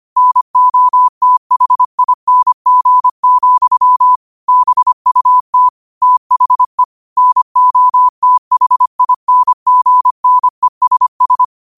Morse code, heard as a keyed tone, about 25 wpm, 1000 Hz. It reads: NOTHING, BUT THE NOTHINGNESS
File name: qotdfiles / 25wpm